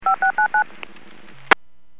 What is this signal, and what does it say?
DTMF, heard as telephone touch tones, tones of about 110 ms. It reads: 5699
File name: WPS / 00845 21-12 WPS 451.25 WPS